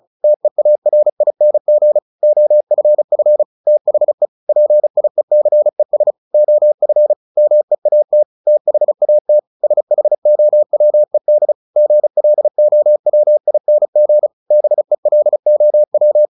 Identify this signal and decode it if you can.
Morse code, heard as a keyed tone, about 34 wpm, 600 Hz. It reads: TEARING OFF THE PIECES OF MEAT THAT SHOWED GLOWING BELOW
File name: the old man and the sea 2 35wpm